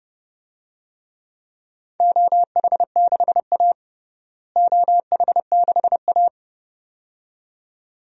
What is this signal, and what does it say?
Morse code, heard as a keyed tone, about 32 wpm, 700 Hz. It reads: OH6A OH6A
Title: DXCC-100 Call Signs - 4 Characters - Repeated 2x - 30wpm